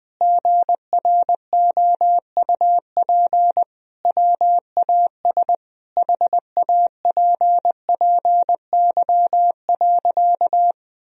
Morse code, heard as a keyed tone, about 21 wpm, 700 Hz. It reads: GROUP WAS HAPPY.
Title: Sentences from Top 1000 Words - Review Part 1 - 20wpm